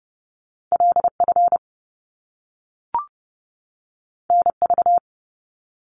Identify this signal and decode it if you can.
Morse code, heard as a keyed tone, about 30 wpm, 700 Hz. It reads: LF NV
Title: Top Two Letter Combinations in Words 30wpm